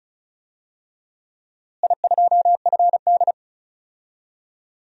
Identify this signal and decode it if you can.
Morse code, heard as a keyed tone, about 36 wpm, 700 Hz. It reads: I2FD
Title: DXCC-100 Call Signs - 4 Characters - Repeated 3x - 35wpm